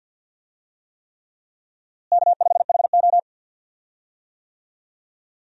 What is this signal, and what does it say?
Morse code, heard as a keyed tone, about 52 wpm, 700 Hz. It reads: K5HO
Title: USA Call Signs - 4 Characters - Repeated 3x - 50wpm